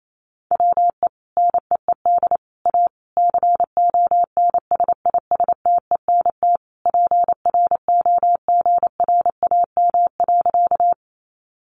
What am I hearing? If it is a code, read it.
Morse code, heard as a keyed tone, about 28 wpm, 700 Hz. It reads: WE NEED A CONSISTENT PROGRAM.
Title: Sentences from Top 1500 Words 28wpm